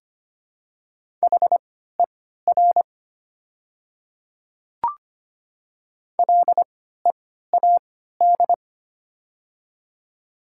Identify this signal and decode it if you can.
Morse code, heard as a keyed tone, about 25 wpm, spaced as for 14 wpm, 700 Hz. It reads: HER LEAD
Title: 3x Character Spacing - Top 500 Words - 25wpm